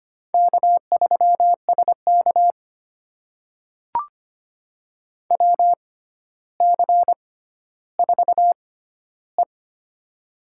Morse code, heard as a keyed tone, about 25 wpm, spaced as for 13 wpm, 700 Hz. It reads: K3SK WC4E
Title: USA Call Signs - 4 Characters - 6x Character Spacing - 25wpm